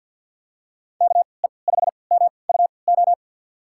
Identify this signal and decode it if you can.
Morse code, heard as a keyed tone, about 48 wpm, spaced as for 28 wpm, 700 Hz. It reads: KE5MUO
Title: USA Call Signs - 6 Characters - 3x Character Spacing - 50wpm